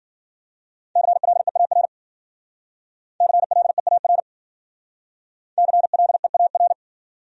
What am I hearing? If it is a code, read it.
Morse code, heard as a keyed tone, about 52 wpm, 700 Hz. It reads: CLEAR CLEAR CLEAR